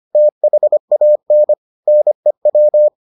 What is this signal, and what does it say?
Morse code, heard as a keyed tone, about 25 wpm, 600 Hz. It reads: THAN NEW